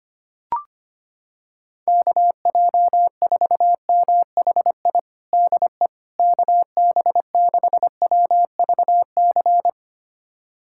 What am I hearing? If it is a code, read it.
Morse code, heard as a keyed tone, about 25 wpm, 700 Hz. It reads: KJ4MHI DE KB6WVC
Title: USA Callsign DE Callsign - 6 Characters - 25wpm